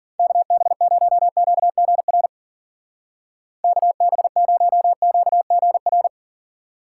Morse code, heard as a keyed tone, about 50 wpm, 700 Hz. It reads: KB0QGR KB0QGR
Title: Speed Racing - USA Call Signs 40wpm